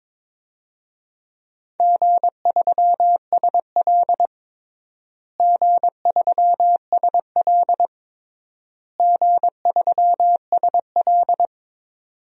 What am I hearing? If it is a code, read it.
Morse code, heard as a keyed tone, about 21 wpm, 700 Hz. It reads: G3SL G3SL G3SL
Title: DXCC-100 Call Signs - 4 Characters - Repeated 3x - 22wpm